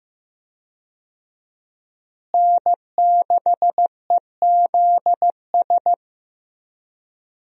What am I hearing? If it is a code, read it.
Morse code, heard as a keyed tone, about 15 wpm, 700 Hz. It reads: N6EZS
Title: USA Call Signs - 5 Characters - Repeated 2x - 15wpm